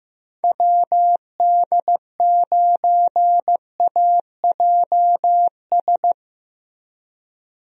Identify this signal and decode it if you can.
Morse code, heard as a keyed tone, about 15 wpm, 700 Hz. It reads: WD9AJS
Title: USA Call Signs - 6 Characters - Repeated 3x - 15wpm